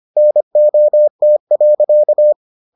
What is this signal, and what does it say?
Morse code, heard as a keyed tone, about 25 wpm, 600 Hz. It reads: NOT.